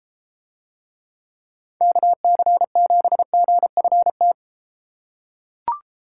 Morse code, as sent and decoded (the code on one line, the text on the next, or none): -.- -.-. --... --. ..-. -
KC7GFT